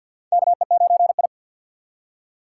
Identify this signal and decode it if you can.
Morse code, heard as a keyed tone, about 48 wpm, 700 Hz. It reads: KE9I
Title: K1USN SST QSO Practice - 50wpm